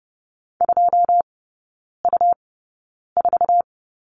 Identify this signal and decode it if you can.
Morse code, heard as a keyed tone, about 30 wpm, spaced as for 9 wpm, 700 Hz. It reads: 2U4
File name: Lesson 17e - Letter u - Three Characters 30wpm